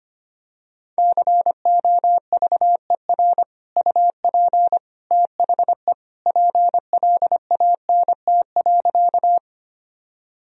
Morse code, heard as a keyed tone, about 25 wpm, 700 Hz. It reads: COVER UP THE PLANT.
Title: Sentences from Top 1000 Words - Review Part 1 - 25wpm